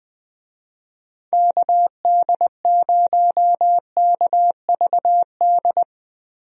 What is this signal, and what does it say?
Morse code, heard as a keyed tone, about 20 wpm, 700 Hz. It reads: KD0KVD
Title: USA Call Signs - 6 Characters - Repeated 2x - 20wpm